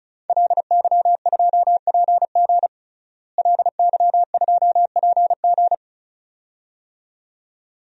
Morse code, heard as a keyed tone, about 34 wpm, 700 Hz. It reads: LY2PG LY2PG
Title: DXCC-100 Call Signs - 5 Characters - Repeated 2x - 35wpm